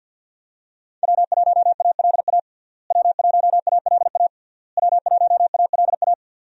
Morse code, heard as a keyed tone, about 50 wpm, 700 Hz. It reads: W1ALA W1ALA W1ALA
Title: USA Call Signs - 5 Characters - Repeated 3x - 50wpm